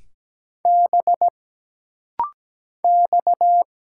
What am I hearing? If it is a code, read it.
Morse code, heard as a keyed tone, about 17 wpm, 700 Hz. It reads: B X
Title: Single Letter-Number Alt - Rapid-Fire 17wpm